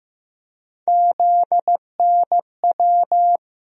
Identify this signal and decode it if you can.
Morse code, heard as a keyed tone, about 15 wpm, 700 Hz. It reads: ZNW
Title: USA Call Sign Postfixes - 3 Letters 15wpm